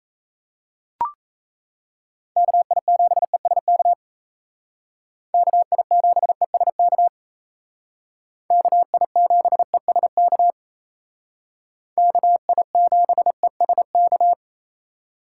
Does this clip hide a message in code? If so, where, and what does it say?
Yes: Morse code from 2.4 to 14.3 s, heard as a keyed tone, about 37 wpm, 700 Hz. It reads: KI7ESK KI7ESK KI7ESK KI7ESK